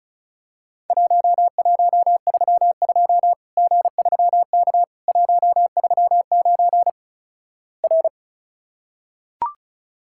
Morse code, heard as a keyed tone, about 36 wpm, 700 Hz. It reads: 1132 G3K 139 R
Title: Worked All Europe DX Contest - QTC Exchange - 35wpm